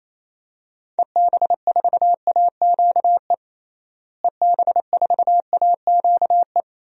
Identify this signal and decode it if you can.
Morse code, heard as a keyed tone, about 28 wpm, 700 Hz. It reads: EB4AQE EB4AQE
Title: DXCC-100 Call Signs - 6 Characters - Repeated 2x - 28wpm